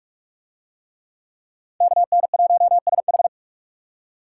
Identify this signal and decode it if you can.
Morse code, heard as a keyed tone, about 45 wpm, 700 Hz. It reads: KN1SH